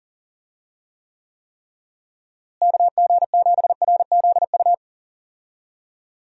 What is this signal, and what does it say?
Morse code, heard as a keyed tone, about 40 wpm, 700 Hz. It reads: KG7RZU
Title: USA Call Signs - 6 Characters - Repeated 3x - 40wpm